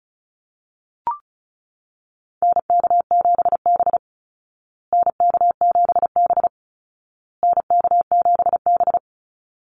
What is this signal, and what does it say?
Morse code, heard as a keyed tone, about 34 wpm, 700 Hz. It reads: NK7B NK7B NK7B
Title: Call Signs Repeated 2x - 35wpm